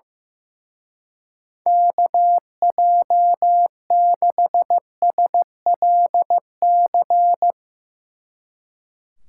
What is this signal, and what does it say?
Morse code, heard as a keyed tone, about 15 wpm, 700 Hz. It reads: KJ6SLC
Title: Speed Racing - USA 2x3 Call Signs 15wpm